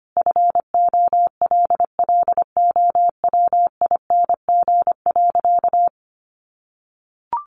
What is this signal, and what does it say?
Morse code, heard as a keyed tone, about 25 wpm, 700 Hz. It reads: FOLLOWING.
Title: Sentences from Top 1300 Words 25wpm